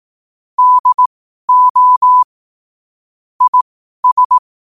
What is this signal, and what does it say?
Morse code, heard as a keyed tone, about 18 wpm, spaced as for 12 wpm, 1000 Hz. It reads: DO IS
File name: qotdfiles / 12wpm